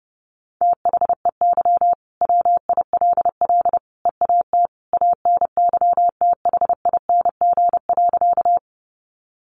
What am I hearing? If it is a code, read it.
Morse code, heard as a keyed tone, about 30 wpm, 700 Hz. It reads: THEY WILL EAT ANYTHING.
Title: Sentences from Top 1000 Words 30wpm